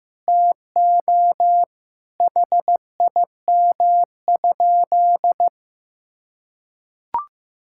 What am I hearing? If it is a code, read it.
Morse code, heard as a keyed tone, about 15 wpm, 700 Hz. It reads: TO HIM?
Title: Sentences from Top 1000 Words - Review Part 6 - 15wpm